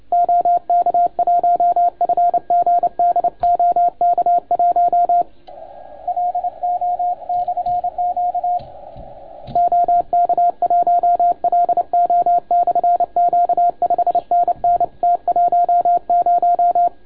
Morse code, heard as a keyed tone, about 29 wpm, 690 Hz, with radio noise. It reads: OK1FGD OK1LO/Q OK1LO/Q 5NN T10
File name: Archiv / CUC287 / OK1FGD CUC 287